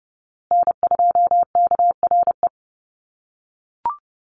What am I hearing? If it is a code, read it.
Morse code, heard as a keyed tone, about 30 wpm, 700 Hz. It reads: N2KRE
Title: USA Call Signs Repeated 2x 30wpm